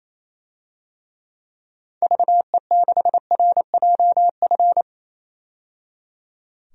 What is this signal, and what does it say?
Morse code, heard as a keyed tone, about 28 wpm, 700 Hz. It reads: VE6RJF
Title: DXCC-100 Call Signs - 6 Characters - 28wpm